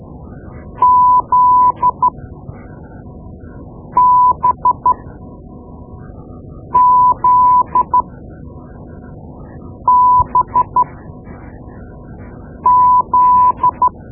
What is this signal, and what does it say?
Morse code, heard as a keyed tone, about 13 wpm, spaced as for 4 wpm, 1000 Hz. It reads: ZBZBZ
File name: Morse Lessons / Morse Lessons / Tape-2B